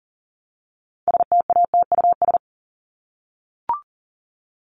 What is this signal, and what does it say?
Morse code, heard as a keyed tone, about 40 wpm, 700 Hz. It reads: STATUS
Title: Top 1251-1500 Words 40wpm